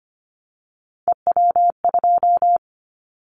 Morse code, heard as a keyed tone, about 25 wpm, 700 Hz. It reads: EW2
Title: Callsign Prefixes for Top 100 DXCC - 25wpm